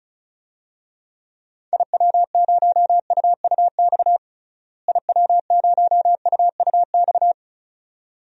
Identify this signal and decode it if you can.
Morse code, heard as a keyed tone, about 36 wpm, 700 Hz. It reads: IW0UUX IW0UUX
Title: DXCC-100 Call Signs - 6 Characters - Repeated 2x - 35wpm